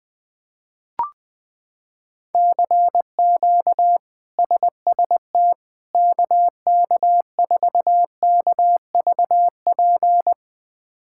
Morse code, heard as a keyed tone, about 20 wpm, 700 Hz. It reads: CQ SST KK4KVP
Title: K1USN SST QSO Practice - 20wpm